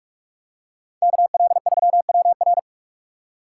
Morse code, heard as a keyed tone, about 45 wpm, 700 Hz. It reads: KL3WR
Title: USA Call Signs - 6x Character Spacing - 45wpm